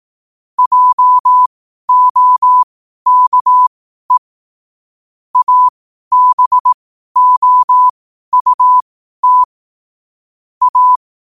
Morse code, heard as a keyed tone, about 18 wpm, spaced as for 12 wpm, 1000 Hz. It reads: JOKE ABOUT A
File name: qotdfiles / 12wpm